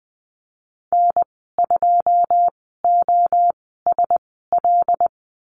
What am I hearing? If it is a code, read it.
Morse code, heard as a keyed tone, about 20 wpm, spaced as for 15 wpm, 700 Hz. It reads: N2OSL
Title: 154 Lesson 25g - Letter p - Call Signs 20wpm